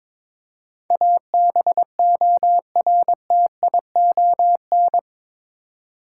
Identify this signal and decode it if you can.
Morse code, heard as a keyed tone, about 22 wpm, 700 Hz. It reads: ABORTION
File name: practice / 22wpm / Eight Letter Words - 22wpm